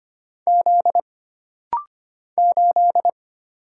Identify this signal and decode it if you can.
Morse code, heard as a keyed tone, about 25 wpm, 700 Hz. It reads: Z 8
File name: Single Character (Letter:Number:Punctuation:Prosign) - Rapid-Fire - 25wpm